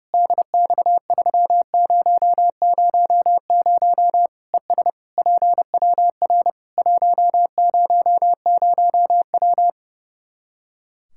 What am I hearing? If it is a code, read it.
Morse code, heard as a keyed tone, about 32 wpm, 700 Hz. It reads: DX3000 ES PWR 100W
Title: Practice QSO - Exchange 2 - Info - 30wpm